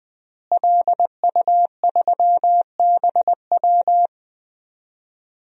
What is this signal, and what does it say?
Morse code, heard as a keyed tone, about 20 wpm, 700 Hz. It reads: LU3BW
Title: DXCC-100 Call Signs - 5 Characters - Repeated 3x - 20wpm